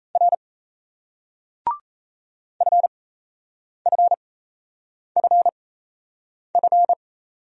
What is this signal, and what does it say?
Morse code, heard as a keyed tone, about 34 wpm, spaced as for 8 wpm, 700 Hz. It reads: R FFFF
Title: Speed-Racing Single Letters 28wpm